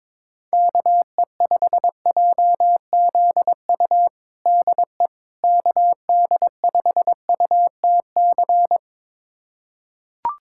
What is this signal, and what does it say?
Morse code, heard as a keyed tone, about 22 wpm, 700 Hz. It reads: KE5JZU DE KD5UTC
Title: USA Callsign DE Callsign 22wpm